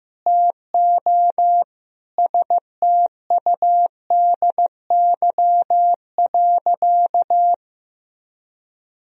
Morse code, heard as a keyed tone, about 15 wpm, 700 Hz. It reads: TO STUDY.
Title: Sentences from Top 1000 Words - Review Part 1 - 15wpm